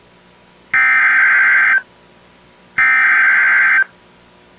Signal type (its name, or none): Emergency Alert System (SAME) tones